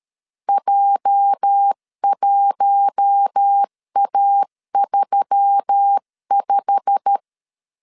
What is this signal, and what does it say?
Morse code, heard as a keyed tone, about 13 wpm, 800 Hz. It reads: J1A35
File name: LEZIONI / 8 YJ45 / 8 YJ45 12wpm 2151